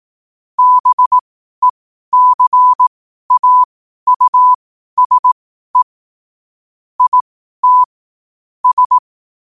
Morse code, heard as a keyed tone, about 18 wpm, spaced as for 12 wpm, 1000 Hz. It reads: BECAUSE IT S
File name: qotdfiles / 12wpm